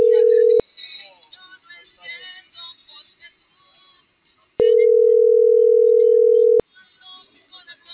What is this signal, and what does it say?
Ringback tone (North America), 440 + 480 Hz.
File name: CENTRAL MABE / out celeste.alcantara